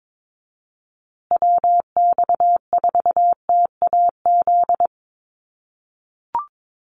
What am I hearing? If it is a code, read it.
Morse code, heard as a keyed tone, about 22 wpm, 700 Hz. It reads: WX4TAZ
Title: USA Call Signs Repeated 2x 22wpm